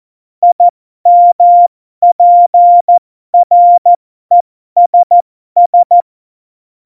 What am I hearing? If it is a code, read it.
Morse code, heard as a keyed tone, about 14 wpm, spaced as for 12 wpm, 700 Hz. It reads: IMPRESS
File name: Morsekurs / Witze